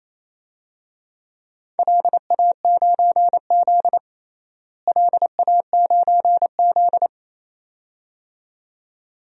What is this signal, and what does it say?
Morse code, heard as a keyed tone, about 28 wpm, 700 Hz. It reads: LA9Z LA9Z
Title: DXCC-100 Call Signs - 4 Characters - Repeated 2x - 28wpm